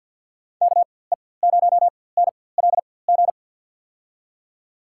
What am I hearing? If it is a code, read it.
Morse code, heard as a keyed tone, about 53 wpm, spaced as for 24 wpm, 700 Hz. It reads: KE0NLG
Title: USA Call Signs - 4x Character Spacing - 50wpm